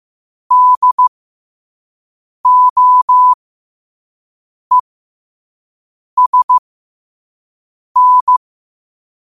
Morse code, heard as a keyed tone, about 15 wpm, spaced as for 5 wpm, 1000 Hz. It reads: DOESN
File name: qotdfiles / 5wpm